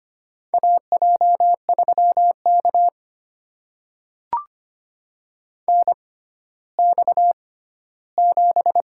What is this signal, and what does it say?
Morse code, heard as a keyed tone, about 25 wpm, spaced as for 14 wpm, 700 Hz. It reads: AJ3K NX7
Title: USA Call Signs - 4 Characters - 6x Character Spacing - 25wpm